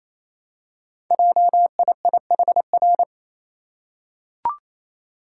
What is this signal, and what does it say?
Morse code, heard as a keyed tone, about 29 wpm, 700 Hz. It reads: JIIHR
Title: Code Groups - 5 Letters - 28wpm